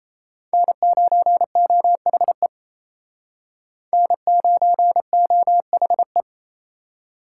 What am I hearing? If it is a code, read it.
Morse code, heard as a keyed tone, about 37 wpm, 700 Hz. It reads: N9OHE N9OHE